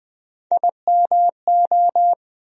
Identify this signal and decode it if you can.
Morse code, heard as a keyed tone, about 21 wpm, 700 Hz. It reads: IMO
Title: USA Call Sign Postfixes - 3 Letters 20wpm